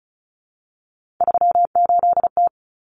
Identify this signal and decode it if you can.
Morse code, heard as a keyed tone, about 34 wpm, 700 Hz. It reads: 38T
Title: Common Cut Numbers - 3-Digit 35wpm